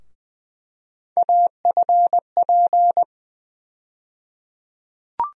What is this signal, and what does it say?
Morse code, heard as a keyed tone, about 20 wpm, 700 Hz. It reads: AFP
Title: USA Call Sign Postfixes - 3 Letters 20wpm